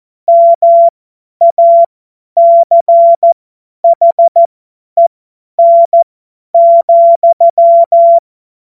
Morse code, heard as a keyed tone, about 14 wpm, spaced as for 10 wpm, 680 Hz. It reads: MACHEN,
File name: Morsekurs / Witze / Witz 24. Januar 2023 50 F70